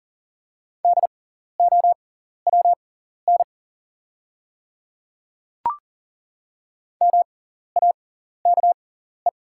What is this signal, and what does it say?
Morse code, heard as a keyed tone, about 40 wpm, spaced as for 14 wpm, 690 Hz. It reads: DOWN MAKE
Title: 6x Character Spacing - Top 100 Words - 40wpm